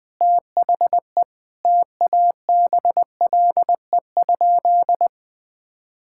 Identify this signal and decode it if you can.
Morse code, heard as a keyed tone, about 20 wpm, 700 Hz. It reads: THE TABLE?
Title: Sentences from Top 1000 Words - Review Part 5 - 20wpm